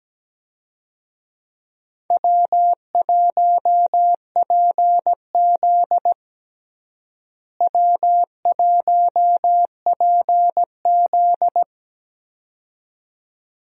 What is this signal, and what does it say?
Morse code, heard as a keyed tone, about 17 wpm, 700 Hz. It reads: W1PZ W1PZ
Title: USA Call Signs - 4 Characters - Repeated 2x - 17wpm